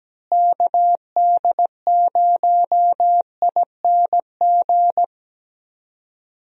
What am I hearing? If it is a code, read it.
Morse code, heard as a keyed tone, about 17 wpm, 700 Hz. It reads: KD0ING
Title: USA Call Signs - 6 Characters - Repeated 2x - 17wpm